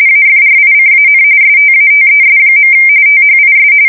RTTY, about 45.45 baud. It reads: AFRICAN STATES. THIS
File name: fsk170 / usnavy1